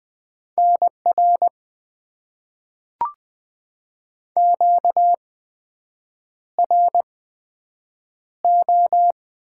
Morse code, heard as a keyed tone, about 20 wpm, spaced as for 7 wpm, 700 Hz. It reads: NR QRO
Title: Top 90 QSO Elements - 8x Character Spacing - 20wpm